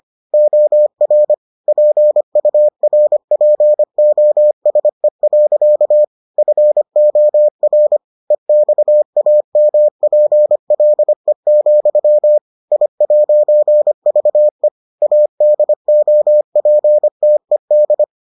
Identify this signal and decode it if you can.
Morse code, heard as a keyed tone, about 25 wpm, 600 Hz. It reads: OR PURPOSE. FOR EXAMPLE, I'VE ADOPTED